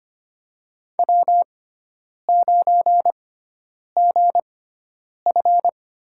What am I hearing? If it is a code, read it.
Morse code, heard as a keyed tone, about 25 wpm, spaced as for 9 wpm, 700 Hz. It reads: W9GF
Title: USA Call Signs - 6x Character Spacing - 25wpm